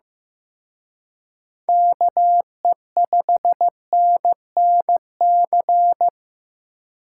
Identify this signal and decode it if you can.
Morse code, heard as a keyed tone, about 18 wpm, 700 Hz. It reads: KE5NNC
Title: Speed Racing - USA 2x3 Call Signs 15wpm